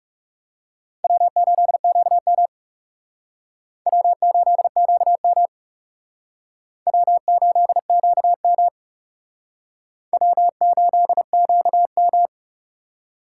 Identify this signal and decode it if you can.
Morse code, heard as a keyed tone, about 37 wpm, 700 Hz. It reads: W8QM W8QM W8QM W8QM
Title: Speed Racing - USA Call Signs 30wpm